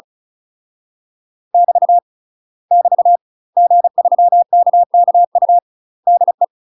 Morse code, heard as a keyed tone, about 37 wpm, 700 Hz. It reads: <BT> <BT> G3KKU DE
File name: UK MASTER QSOs TRAINING / 35WPM / UK MASTER QSOs 0030